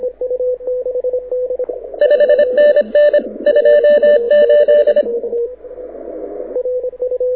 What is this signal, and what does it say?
Morse code, heard as a keyed tone, about 26 wpm, 590 Hz, with radio noise. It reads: 5NN 28
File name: audio zawody / IARU 2015 CW SN7Q